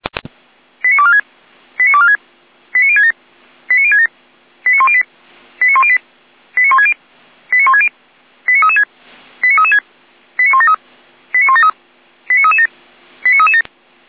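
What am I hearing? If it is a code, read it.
Five-tone selective call (ZVEI), heard as quick runs of five tones, tones of about 70 ms. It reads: 89246 89246 89067 89067 89108 89108 89160 89160 89336 89336 89173 89173 89337 89337
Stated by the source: ZVEI five-tone call 89078 received by SDR, channel 455.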